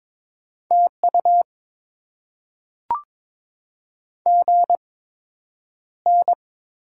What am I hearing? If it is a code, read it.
Morse code, heard as a keyed tone, about 22 wpm, spaced as for 9 wpm, 700 Hz. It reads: TU GN
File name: Top 90 QSO Elements - 8x Character Spacing - 22wpm